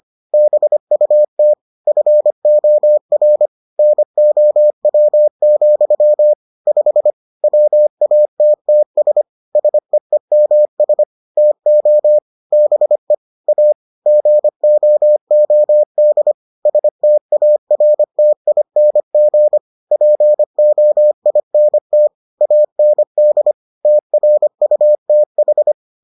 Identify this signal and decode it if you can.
Morse code, heard as a keyed tone, about 25 wpm, 600 Hz. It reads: BUT FOR NOW, 5 WATTS SEEMS TO BE A GOOD STARTING POINT AND TRUTH